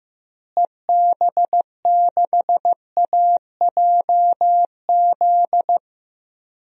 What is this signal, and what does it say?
Morse code, heard as a keyed tone, about 15 wpm, 700 Hz. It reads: EB6AJZ
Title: DXCC-100 Call Signs - 6 Characters - Repeated 3x - 15wpm